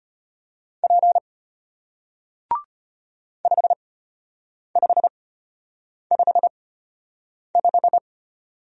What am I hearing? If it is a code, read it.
Morse code, heard as a keyed tone, about 33 wpm, spaced as for 8 wpm, 700 Hz. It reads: P 5555